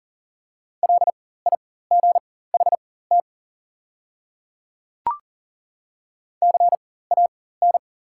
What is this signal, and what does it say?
Morse code, heard as a keyed tone, about 40 wpm, spaced as for 19 wpm, 700 Hz. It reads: LIGHT CAN
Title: 4x Character Spacing - Top 300 Words - 40wpm